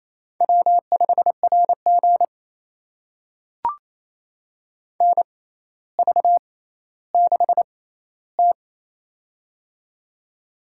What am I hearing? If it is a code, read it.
Morse code, heard as a keyed tone, about 28 wpm, spaced as for 14 wpm, 700 Hz. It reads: W5RG NV6T
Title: USA Call Signs - 4 Characters - 6x Character Spacing - 28wpm